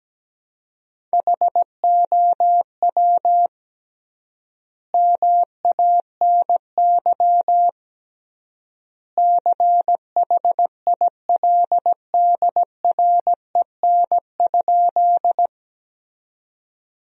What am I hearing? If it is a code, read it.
Morse code, heard as a keyed tone, about 17 wpm, 700 Hz. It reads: HOW MANY CHILDREN?
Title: 3x Word-Spacing - Sentences from Top 300 Words - 17wpm